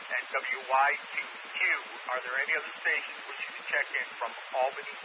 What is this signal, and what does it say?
Emergency Alert System (SAME) tones, heard as harsh buzzing bursts, with speech. It is heard from 0.9 to 4.1 s.